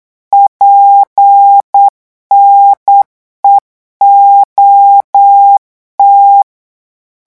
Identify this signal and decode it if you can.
Morse code, heard as a keyed tone, about 8 wpm, 800 Hz. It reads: PNEOT
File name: morse mp3 / revision8-vitesse 8 mots